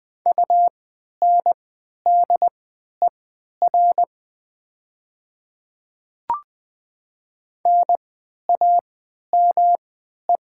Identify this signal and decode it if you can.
Morse code, heard as a keyed tone, about 20 wpm, spaced as for 11 wpm, 700 Hz. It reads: UNDER NAME
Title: 3x Character Spacing - Top 300 Words - 20wpm